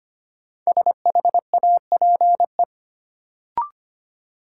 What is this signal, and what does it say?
Morse code, heard as a keyed tone, about 25 wpm, 700 Hz. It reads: SHAPE
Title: Sentences from Top 500 Words - New Word Drill - 25wpm